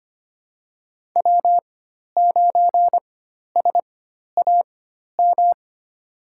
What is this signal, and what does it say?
Morse code, heard as a keyed tone, about 25 wpm, spaced as for 12 wpm, 700 Hz. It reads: W9SAM
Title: USA Call Signs - 4x Character Spacing - 25wpm